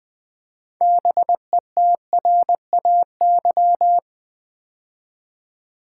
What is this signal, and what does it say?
Morse code, heard as a keyed tone, about 20 wpm, 700 Hz. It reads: BETRAY
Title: Build A Word - 3 Letters At A Time - 6-Letter Words - 20wpm